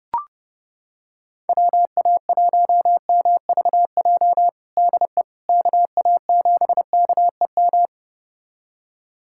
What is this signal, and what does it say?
Morse code, heard as a keyed tone, about 30 wpm, 700 Hz. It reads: WA1MVJ DE KA7KEM
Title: USA Callsign DE Callsign - 6 Characters - 30wpm